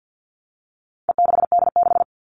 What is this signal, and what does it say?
Morse code, heard as a keyed tone, about 48 wpm, 700 Hz. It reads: E6D6
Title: All Cut Numbers - 4-Digit 50wpm